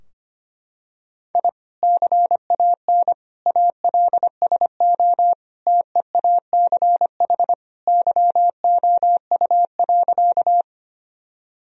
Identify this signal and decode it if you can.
Morse code, heard as a keyed tone, about 25 wpm, 700 Hz. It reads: I CAN ALSO TEACH YOU.
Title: Sentences from Top 1000 Words - Review Part 2 - 25wpm